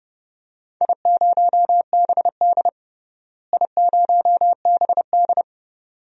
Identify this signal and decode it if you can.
Morse code, heard as a keyed tone, about 30 wpm, 700 Hz. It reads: I0BD I0BD
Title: DXCC-100 Call Signs - 4 Characters - Repeated 2x - 30wpm